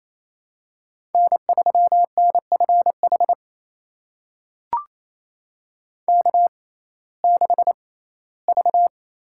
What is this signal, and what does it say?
Morse code, heard as a keyed tone, about 28 wpm, spaced as for 17 wpm, 700 Hz. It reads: N3NFH K6V
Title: USA Call Signs - 5 Characters - 6x Character Spacing - 28wpm